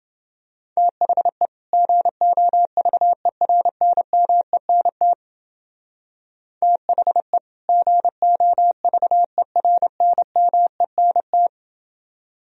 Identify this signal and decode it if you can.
Morse code, heard as a keyed tone, about 29 wpm, 700 Hz. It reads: THE GOVERNMENT THE GOVERNMENT